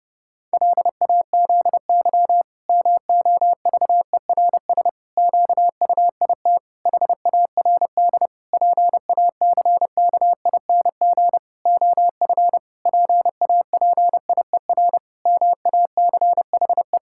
Morse code, heard as a keyed tone, about 30 wpm, 700 Hz. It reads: LAZY MOVERS QUIT HARD PACKING OF PAPIER MACHE
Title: Pangram Challenge v2 - 30wpm